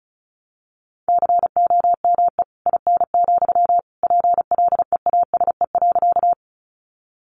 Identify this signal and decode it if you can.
Morse code, heard as a keyed tone, about 34 wpm, 700 Hz. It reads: COME IN, PLEASE.
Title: Sentences from Top 1500 Words 35wpm